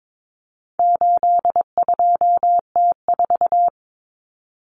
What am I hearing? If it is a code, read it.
Morse code, heard as a keyed tone, about 22 wpm, 700 Hz. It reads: 82T4
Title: Common Cut Numbers - 4-Digit 22wpm